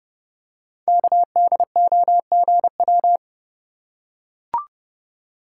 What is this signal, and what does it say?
Morse code, heard as a keyed tone, about 30 wpm, 700 Hz. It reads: KDOGW
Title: Code Groups - 5 Letters - 30wpm